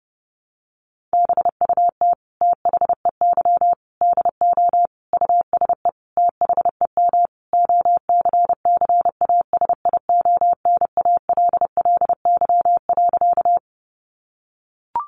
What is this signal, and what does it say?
Morse code, heard as a keyed tone, about 30 wpm, 700 Hz. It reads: BUT THEY DO USE THEM OCCASIONALLY.
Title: Sentences from Top 1300 Words 30wpm